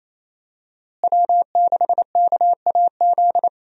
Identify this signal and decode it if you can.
Morse code, heard as a keyed tone, about 28 wpm, 700 Hz. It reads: W6KAZ
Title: USA Call Signs - 5 Characters - Repeated 3x - 28wpm